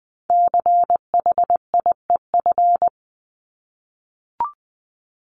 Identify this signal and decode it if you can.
Morse code, heard as a keyed tone, about 20 wpm, 700 Hz. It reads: CHIEF
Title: LY Suffix - Top 100 Words with Base Words 20wpm